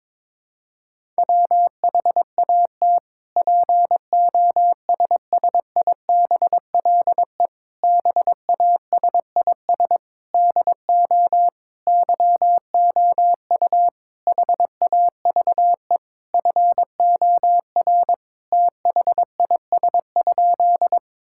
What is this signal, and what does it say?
Morse code, heard as a keyed tone, about 22 wpm, 700 Hz. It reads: WHAT POSSIBLE BASIS DO YOU HAVE FOR THIS?
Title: Sentences from Top 1000 Words - Review Part 5 - 22wpm